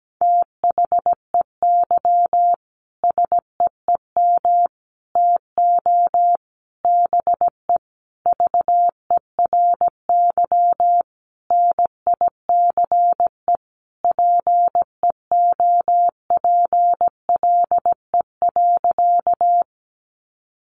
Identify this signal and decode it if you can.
Morse code, heard as a keyed tone, about 17 wpm, 700 Hz. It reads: THEY SEEM TO BE VERY NICE PEOPLE.
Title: Sentences from Top 1100 Words 17wpm